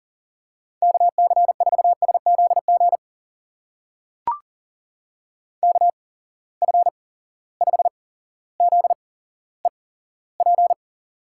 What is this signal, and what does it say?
Morse code, heard as a keyed tone, about 41 wpm, spaced as for 17 wpm, 700 Hz. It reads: KC4SZG KF5ZEP
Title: USA Call Signs - 6 Characters - 8x Character Spacing - 40wpm